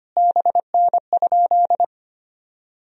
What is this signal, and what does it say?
Morse code, heard as a keyed tone, about 25 wpm, 700 Hz. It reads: BN?
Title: Sentences from Top 400 Words and Abbreviations 25wpm